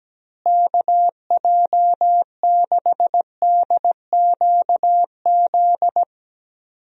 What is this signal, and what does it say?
Morse code, heard as a keyed tone, about 17 wpm, 700 Hz. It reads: KJ6DQZ
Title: USA Call Signs - 6 Characters - Repeated 2x - 17wpm